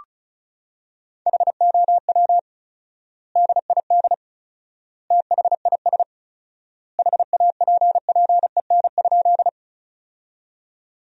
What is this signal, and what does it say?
Morse code, heard as a keyed tone, about 36 wpm, 700 Hz. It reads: HOW DID THIS HAPPEN?
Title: 4x Word-Spacing - Sentences from Top 400 Words - 35wpm